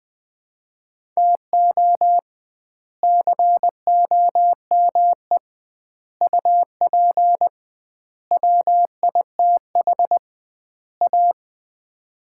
Morse code, heard as a keyed tone, about 20 wpm, 700 Hz. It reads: TO COME UP WITH A
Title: 2x Word-Spacing - Sets of 5 Words - 20wpm